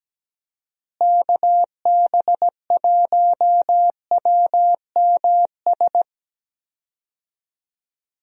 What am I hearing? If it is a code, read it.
Morse code, heard as a keyed tone, about 17 wpm, 700 Hz. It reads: KB1WMS
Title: USA Call Signs - 6 Characters - Repeated 2x - 17wpm